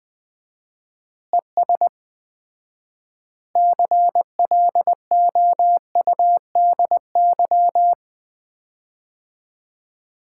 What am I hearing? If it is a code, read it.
Morse code, heard as a keyed tone, about 20 wpm, 700 Hz. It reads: ES CLOUDY
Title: Common QSO Phrases - 4x Word-Spacing - 20wpm